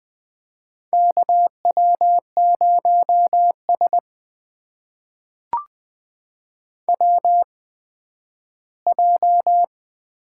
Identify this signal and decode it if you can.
Morse code, heard as a keyed tone, about 20 wpm, spaced as for 12 wpm, 700 Hz. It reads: KW0S WJ
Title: USA Call Signs - 8x Character Spacing - 20wpm